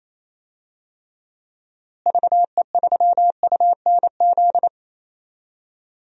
Morse code, heard as a keyed tone, about 28 wpm, 700 Hz. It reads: VE3UNZ
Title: DXCC-100 Call Signs - 6 Characters - Repeated 2x - 28wpm